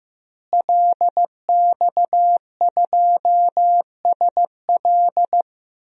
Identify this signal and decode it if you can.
Morse code, heard as a keyed tone, about 15 wpm, 700 Hz. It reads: LX2SL
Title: DXCC-100 Call Signs - 5 Characters - Repeated 3x - 15wpm